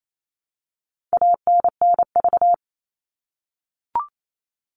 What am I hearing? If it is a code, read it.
Morse code, heard as a keyed tone, about 28 wpm, 700 Hz. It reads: ANNV
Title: All Cut Numbers - 4-Digit 28wpm